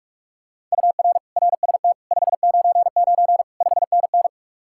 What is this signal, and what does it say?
Morse code, heard as a keyed tone, about 48 wpm, 700 Hz. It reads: UR RST 599 5NN